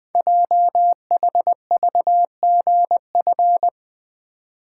Morse code, heard as a keyed tone, about 21 wpm, 700 Hz. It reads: JHVGF
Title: Code Groups - 5 Letters - 20wpm